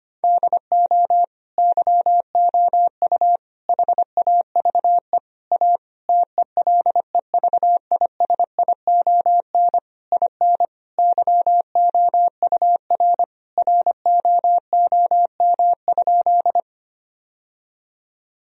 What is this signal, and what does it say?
Morse code, heard as a keyed tone, about 25 wpm, 700 Hz. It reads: DO YOU HAVE A TELEVISION IN YOUR ROOM?
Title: Sentences from Top 1000 Words - Review Part 4 - 25wpm